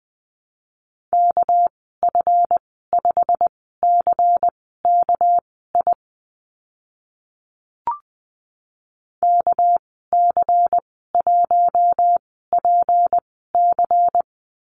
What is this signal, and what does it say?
Morse code, heard as a keyed tone, about 20 wpm, spaced as for 14 wpm, 700 Hz. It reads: KF5CKI KC1PC
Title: 200 Lesson 32g - Letter k - Call Signs 20wpm